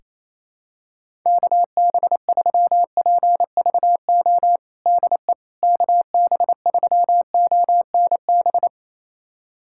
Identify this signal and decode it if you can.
Morse code, heard as a keyed tone, about 28 wpm, 700 Hz. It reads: KB3PVO DE KB3ONB
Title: USA Callsign DE Callsign - 6 Characters - 28wpm